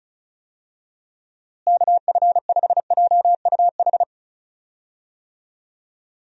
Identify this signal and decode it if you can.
Morse code, heard as a keyed tone, about 34 wpm, 700 Hz. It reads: KF5JUH
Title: USA Call Signs - 6 Characters - Repeated 2x - 35wpm